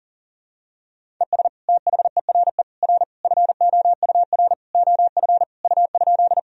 Morse code, heard as a keyed tone, about 40 wpm, 700 Hz. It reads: ES THERE R FOUR OF U?
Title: Sentences from Top 300 Words and Abbreviations 40wpm